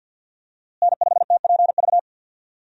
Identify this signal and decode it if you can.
Morse code, heard as a keyed tone, about 52 wpm, 700 Hz. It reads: N5TPV
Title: USA Call Signs - 5 Characters - Repeated 2x - 50wpm